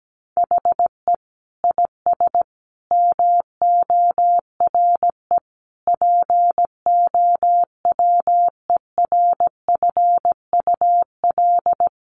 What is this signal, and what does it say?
Morse code, heard as a keyed tone, about 17 wpm, 700 Hz. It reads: HE IS MORE POWERFUL
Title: Sentences from Top 1300 Words 17wpm